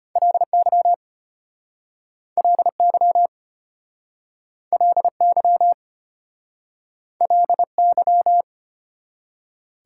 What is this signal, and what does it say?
Morse code, heard as a keyed tone, about 33 wpm, 700 Hz. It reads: LY LY LY LY